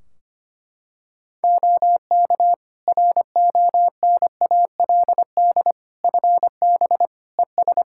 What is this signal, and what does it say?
Morse code, heard as a keyed tone, about 25 wpm, 700 Hz. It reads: OK RONALD FB ES
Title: Practice QSO - Exchange 2 - Info - 25wpm